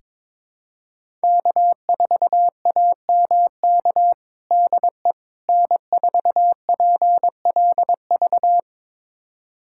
Morse code, heard as a keyed tone, about 22 wpm, 700 Hz. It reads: K4AMK DE N4PLV
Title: USA Callsign DE Callsign - 5 Characters - 22wpm